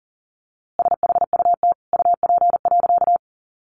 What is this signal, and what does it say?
Morse code, heard as a keyed tone, about 40 wpm, 700 Hz. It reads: SHUT UP.
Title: A Sound Of Thunder 40wpm